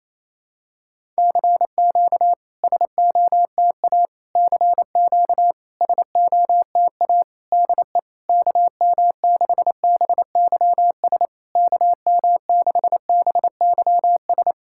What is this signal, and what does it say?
Morse code, heard as a keyed tone, about 28 wpm, 700 Hz. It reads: CQ SOTA CQ SOTA DE KM6BYS KM6BYS
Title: SOTA QSO Practice 28wpm